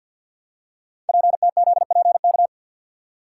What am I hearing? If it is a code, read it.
Morse code, heard as a keyed tone, about 46 wpm, 700 Hz. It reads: PTZPK
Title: Code Groups - 5 Letters - 50wpm